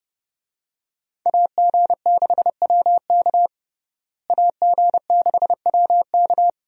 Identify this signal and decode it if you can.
Morse code, heard as a keyed tone, about 32 wpm, 700 Hz. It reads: AG6WK AG6WK
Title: USA Call Signs - 5 Characters - Repeated 3x - 30wpm